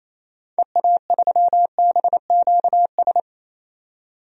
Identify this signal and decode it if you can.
Morse code, heard as a keyed tone, about 28 wpm, 700 Hz. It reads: EA3BQS
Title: DXCC-100 Call Signs - 6 Characters - 28wpm